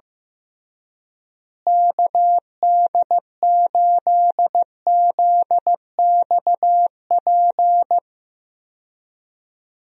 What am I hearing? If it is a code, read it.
Morse code, heard as a keyed tone, about 15 wpm, 700 Hz. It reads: KD8ZXP
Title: USA Call Signs - 6 Characters - Repeated 3x - 15wpm